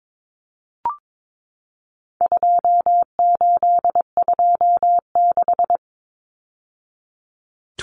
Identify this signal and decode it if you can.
Morse code, heard as a keyed tone, about 22 wpm, 700 Hz. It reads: 2826
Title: Numbers - 4-Digit 22wpm